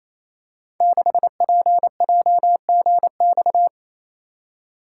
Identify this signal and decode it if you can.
Morse code, heard as a keyed tone, about 29 wpm, 700 Hz. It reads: 6PJGX
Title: Code Groups - 5 Letter:Numbers - 28wpm